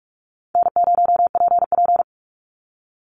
Morse code, heard as a keyed tone, about 44 wpm, 700 Hz. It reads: N0PP
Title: Easy Call Signs Repeated 3x - 45wpm